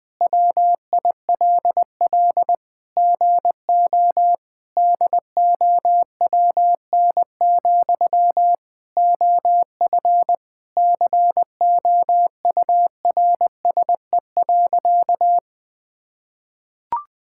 Morse code, heard as a keyed tone, about 20 wpm, 700 Hz. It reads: WILL GO DOWN, OF COURSE.
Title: Sentences from Top 1000 Words - Review Part 5 - 20wpm